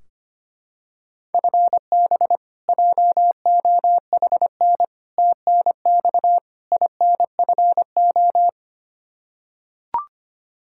Morse code, heard as a keyed tone, about 25 wpm, 700 Hz. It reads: FB JOHN TNX INFO
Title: Practice QSO - Exchange 3 - More Info - 25wpm